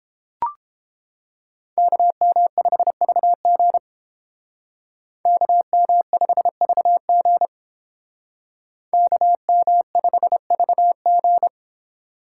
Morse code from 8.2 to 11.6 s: KM5VG

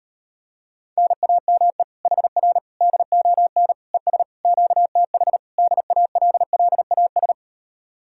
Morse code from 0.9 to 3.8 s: NAME HR DON

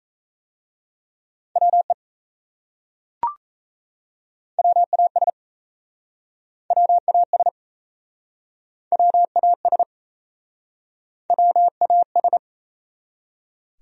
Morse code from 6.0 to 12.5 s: WAS WAS WAS

